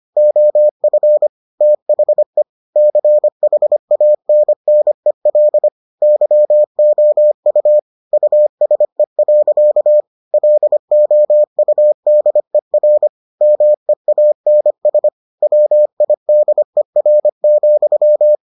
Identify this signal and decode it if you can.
Morse code, heard as a keyed tone, about 25 wpm, 600 Hz. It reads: OF THE CHANNEL YOU USE. LOUDER MEANS WIDER,